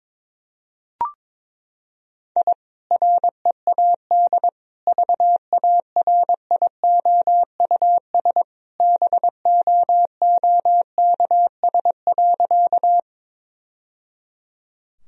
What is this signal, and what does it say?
Morse code, heard as a keyed tone, about 22 wpm, 700 Hz. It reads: I READ VARIOUS BOOKS.
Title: Sentences from Top 1000 Words - Review Part 4 - 22wpm